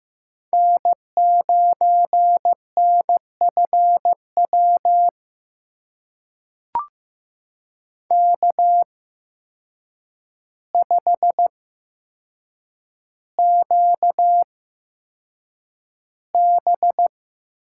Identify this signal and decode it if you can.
Morse code, heard as a keyed tone, about 15 wpm, spaced as for 7 wpm, 700 Hz. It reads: N9NFW K5QB
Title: USA Call Signs - 5 Characters - 8x Character Spacing - 15wpm